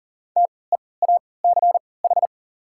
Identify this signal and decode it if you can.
Morse code, heard as a keyed tone, about 40 wpm, spaced as for 23 wpm, 700 Hz. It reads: TEACH
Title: 3x Character Spacing - Top 500 Words - 40wpm